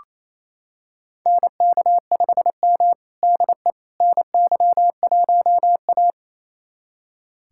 Morse code, heard as a keyed tone, about 28 wpm, 700 Hz. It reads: NK5M DE NY1A
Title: USA Callsign DE Callsign - 4 Characters - 28wpm